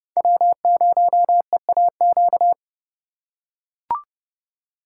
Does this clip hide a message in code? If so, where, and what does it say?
Yes: Morse code from 0.2 to 2.5 s, heard as a keyed tone, about 28 wpm, 700 Hz. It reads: W0EAQ